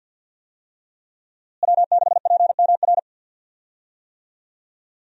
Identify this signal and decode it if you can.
Morse code, heard as a keyed tone, about 46 wpm, 700 Hz. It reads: W6PMR
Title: USA Call Signs - 5 Characters - Repeated 3x - 50wpm